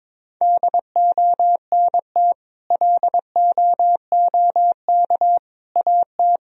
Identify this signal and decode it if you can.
Morse code, heard as a keyed tone, about 22 wpm, 700 Hz. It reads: DONT LOOK AT
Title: Sentences from Top 1000 Words - Review Part 1 - 22wpm